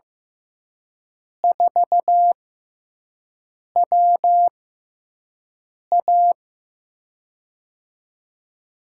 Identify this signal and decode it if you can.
Morse code, heard as a keyed tone, about 15 wpm, spaced as for 5 wpm, 700 Hz. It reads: 4WA
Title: USA Call Signs - 6x Character Spacing - 15wpm